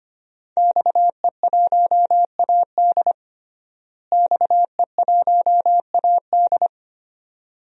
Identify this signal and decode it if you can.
Morse code, heard as a keyed tone, about 25 wpm, 700 Hz. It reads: XE1AD XE1AD
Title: DXCC-100 Call Signs - 5 Characters - Repeated 2x - 25wpm